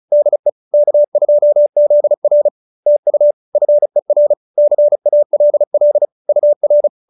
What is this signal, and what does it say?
Morse code, heard as a keyed tone, about 34 wpm, 590 Hz. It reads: DE K2ZR TU FER CALL UR
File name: QSO302 35